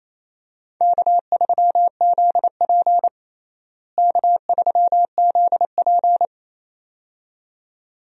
Morse code, heard as a keyed tone, about 28 wpm, 700 Hz. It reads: K3ZP K3ZP
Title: USA Call Signs - 4 Characters - Repeated 2x - 28wpm